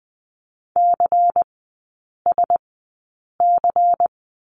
Morse code, heard as a keyed tone, about 20 wpm, spaced as for 8 wpm, 700 Hz. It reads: CSC
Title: 093 Lesson 16e - Letter c - Three Characters 20wpm